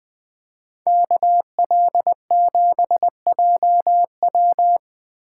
Morse code, heard as a keyed tone, about 20 wpm, 700 Hz. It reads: KL7JW
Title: Speed Racing - USA 2x2 Call Signs 17wpm